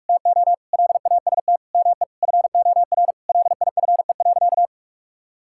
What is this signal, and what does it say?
Morse code, heard as a keyed tone, about 41 wpm, 700 Hz. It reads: TO LAST ME FOR LIFE.
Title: Sentences from Top 1000 Words - Review Part 6 - 45wpm